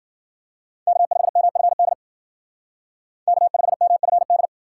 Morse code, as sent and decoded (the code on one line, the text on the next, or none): -.- ..... -- ..-. -.. / -.- ..... -- ..-. -..
K5MFD K5MFD